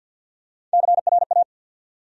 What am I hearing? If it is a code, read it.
Morse code, heard as a keyed tone, about 53 wpm, 700 Hz. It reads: CRA
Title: USA Call Sign Postfixes - 3 Letters 50wpm